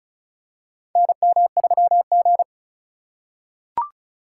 Morse code, heard as a keyed tone, about 36 wpm, 700 Hz. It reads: NM3G
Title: USA Call Signs - 4 Characters - 3x Character Spacing - 35wpm